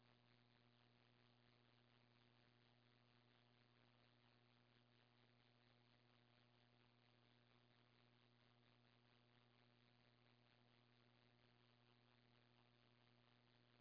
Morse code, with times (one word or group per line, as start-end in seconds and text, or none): none